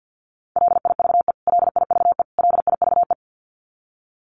Morse code, heard as a keyed tone, about 50 wpm, 700 Hz. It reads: LIVE LIVE LIVE
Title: Top 500 Words Repeated 50wpm